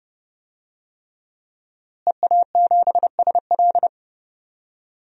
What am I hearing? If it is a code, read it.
Morse code, heard as a keyed tone, about 30 wpm, 700 Hz. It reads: EA7SL
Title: DXCC-100 Call Signs - 5 Characters - Repeated 3x - 30wpm